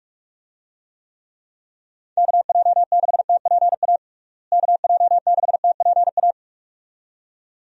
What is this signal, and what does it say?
Morse code, heard as a keyed tone, about 46 wpm, 700 Hz. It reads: KJ6TPA KJ6TPA
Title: USA Call Signs - 6 Characters - Repeated 2x - 45wpm